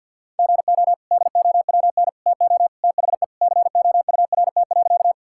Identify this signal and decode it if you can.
Morse code, heard as a keyed tone, about 50 wpm, 700 Hz. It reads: GO DOWN TO THE COURT.
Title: Sentences from Top 1600 Words 50wpm